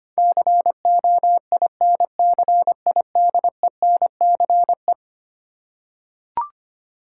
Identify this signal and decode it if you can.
Morse code, heard as a keyed tone, about 25 wpm, 700 Hz. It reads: COINCIDENCE
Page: morse-code-ninja-practice-25wpm